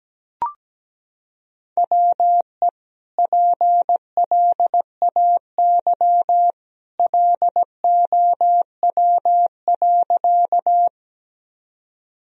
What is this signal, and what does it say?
Morse code, heard as a keyed tone, about 17 wpm, 700 Hz. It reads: WE PLAY LOW.
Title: Sentences from Top 1000 Words - Review Part 1 - 17wpm